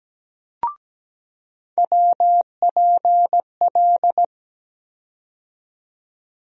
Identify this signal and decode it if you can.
Morse code, heard as a keyed tone, about 17 wpm, 700 Hz. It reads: WPL
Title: USA Call Sign Postfixes - 3 Letters 17wpm